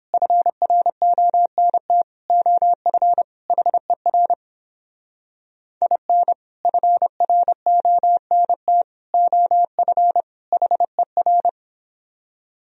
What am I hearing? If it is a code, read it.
Morse code, heard as a keyed tone, about 29 wpm, 700 Hz. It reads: FRONT OF HER IN FRONT OF HER